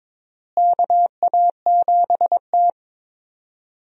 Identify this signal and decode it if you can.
Morse code, heard as a keyed tone, about 22 wpm, 700 Hz. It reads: KA7T
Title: USA Call Signs - 4 Characters - Repeated 2x - 22wpm